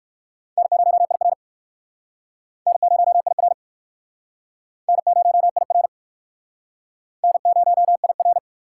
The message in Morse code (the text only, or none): N0IR N0IR N0IR N0IR